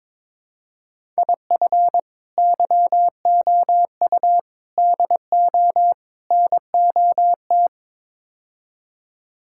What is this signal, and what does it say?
Morse code, heard as a keyed tone, about 22 wpm, 700 Hz. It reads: IF YOU DO NOT
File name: practice / 22wpm / Build A Phrase - 5-Word Phrases - 22wpm